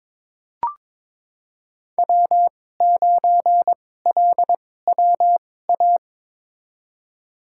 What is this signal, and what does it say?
Morse code, heard as a keyed tone, about 22 wpm, spaced as for 16 wpm, 700 Hz. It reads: W9LWA
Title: USA Call Signs - 5 Characters - 2x Character Spacing - 22wpm